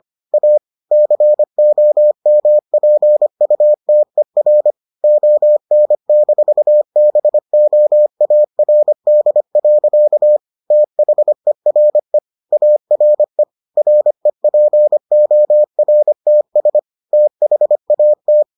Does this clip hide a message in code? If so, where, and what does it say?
Yes: Morse code from 0.3 to 18.4 s, heard as a keyed tone, about 25 wpm, 600 Hz. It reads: A COMPUTER ON-BOARD. THERE ARE REPORTS THAT